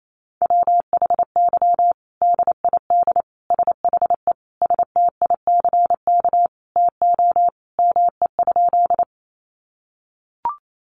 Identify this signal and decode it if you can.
Morse code, heard as a keyed tone, about 28 wpm, 700 Hz. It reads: WHY DID SHE STICK TO ME?
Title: Sentences from Top 1200 Words 28wpm